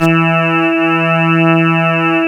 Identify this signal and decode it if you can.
There is no code.